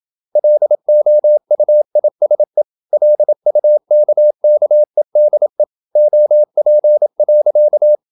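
Morse code, heard as a keyed tone, about 26 wpm, 600 Hz. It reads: LOUISE LUKKEDE OP.
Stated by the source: High-speed (QRQ) Morse at 26 wpm.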